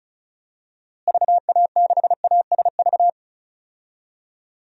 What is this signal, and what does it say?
Morse code, heard as a keyed tone, about 36 wpm, 700 Hz. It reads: VA6ASV
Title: DXCC-100 Call Signs - 6 Characters - Repeated 2x - 35wpm